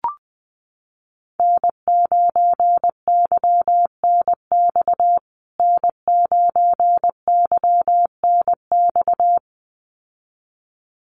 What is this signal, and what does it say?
Morse code, heard as a keyed tone, about 20 wpm, 700 Hz. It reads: N9YNX N9YNX
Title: USA Call Signs Repeated 2x 20wpm